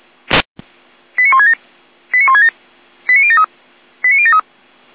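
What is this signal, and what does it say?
Five-tone selective call (ZVEI), heard as quick runs of five tones, tones of about 70 ms. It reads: 89167 89167 89073 89073
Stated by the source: ZVEI five-tone selective call 89119, received by SDR.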